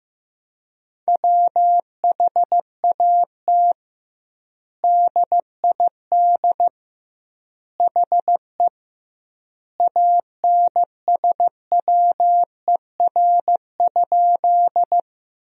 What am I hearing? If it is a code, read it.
Morse code, heard as a keyed tone, about 15 wpm, 700 Hz. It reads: WHAT DID HE ANSWER?
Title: 2x Word-Spacing - Sentences from Top 300 Words - 15wpm